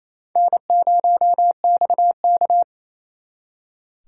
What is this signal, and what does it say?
Morse code, heard as a keyed tone, about 34 wpm, 700 Hz. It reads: N0XK
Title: Speed Racing - USA 1x2 Call Signs 28wpm